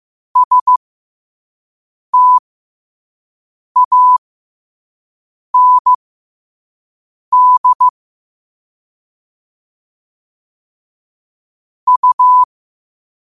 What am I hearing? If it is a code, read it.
Morse code, heard as a keyed tone, about 15 wpm, spaced as for 5 wpm, 1000 Hz. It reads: STAND U
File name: qotdfiles / 5wpm